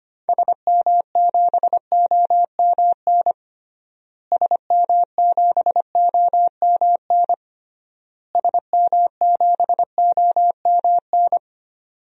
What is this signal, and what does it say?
Morse code, heard as a keyed tone, about 25 wpm, 690 Hz. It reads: SM7OMN SM7OMN SM7OMN
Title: DXCC-100 Call Signs - 6 Characters - Repeated 3x - 25wpm